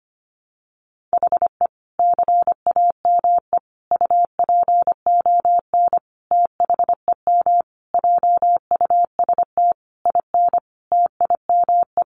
Morse code, heard as a keyed tone, about 25 wpm, 700 Hz. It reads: HE CAME UPON THEM JUST IN TIME
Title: Sentences from Top 1100 Words 25wpm